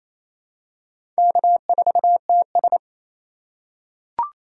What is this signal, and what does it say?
Morse code, heard as a keyed tone, about 28 wpm, 700 Hz. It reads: K4TS
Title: USA Call Signs - 4 Characters - 3x Character Spacing - 28wpm